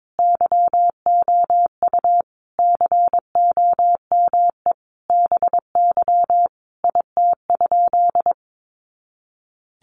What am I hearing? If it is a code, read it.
Morse code, heard as a keyed tone, about 22 wpm, 700 Hz. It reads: YOU COME BY IT?
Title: Sentences from Top 100 Words 22wpm